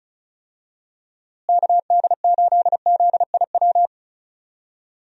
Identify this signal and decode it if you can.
Morse code, heard as a keyed tone, about 34 wpm, 700 Hz. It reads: KD8ZIW
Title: USA Call Signs - 6 Characters - Repeated 2x - 35wpm